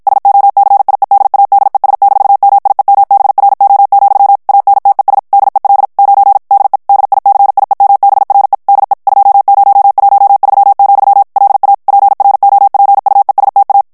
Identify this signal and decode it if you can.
Morse code, heard as a keyed tone, about 52 wpm, 800 Hz. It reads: SOCIEDADES-MIEMBRO, ANTES DEL 9 DE DICIEMBRE DE 2013, LA PROPUESTA
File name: ESPAGNOL TEXTE EN CLAIR 1 VITESSE 50 MOTS MINUTES